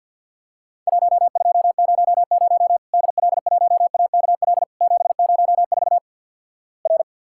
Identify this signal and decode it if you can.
Morse code, heard as a keyed tone, about 52 wpm, 700 Hz. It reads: 1200 DL1AKL 704 R
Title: Worked All Europe DX Contest - QTC Exchange - 50wpm